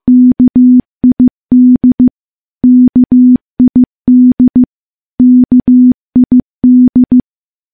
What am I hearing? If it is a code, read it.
Morse code, heard as a keyed tone, about 15 wpm, 250 Hz. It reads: KID KID KID
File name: morse / morse-training-15WPM